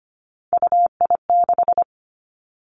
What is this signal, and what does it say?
Morse code, heard as a keyed tone, about 25 wpm, 700 Hz. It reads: UI6
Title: Three Challenging Letter-Number Combinations 25wpm